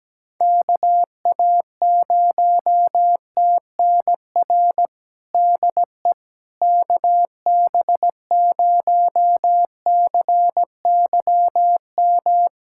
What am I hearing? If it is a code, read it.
Morse code, heard as a keyed tone, about 17 wpm, 700 Hz. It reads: KA0TNR DE KB0CYM
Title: USA Callsign DE Callsign 17wpm